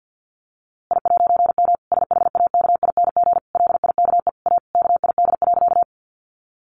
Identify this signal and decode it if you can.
Morse code, heard as a keyed tone, about 50 wpm, 700 Hz. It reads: I'M SHAKING LIKE A KID.
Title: A Sound Of Thunder 50wpm